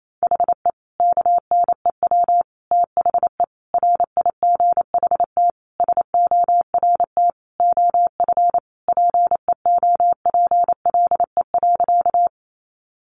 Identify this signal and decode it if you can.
Morse code, heard as a keyed tone, about 28 wpm, 700 Hz. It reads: HE KNEW THE RIGHT SORT OF PEOPLE.
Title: Sentences from Top 1300 Words 28wpm